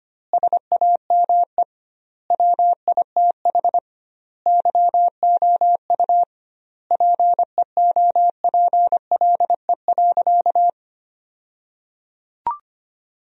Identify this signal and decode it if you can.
Morse code, heard as a keyed tone, about 25 wpm, 700 Hz. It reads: SAME WITH YOU PEOPLE.
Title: 2x Word-Spacing - Sentences from Top 300 Words - 25wpm